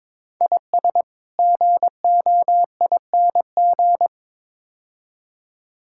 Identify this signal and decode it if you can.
Morse code, heard as a keyed tone, about 21 wpm, 700 Hz. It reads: IS GOING
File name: practice / 22wpm / Build A Phrase - 5-Word Phrases - 22wpm